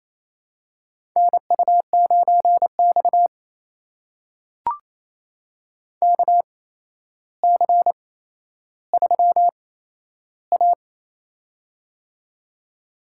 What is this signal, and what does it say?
Morse code, heard as a keyed tone, about 28 wpm, spaced as for 12 wpm, 700 Hz. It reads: NU9X KC3A
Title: USA Call Signs - 8x Character Spacing - 28wpm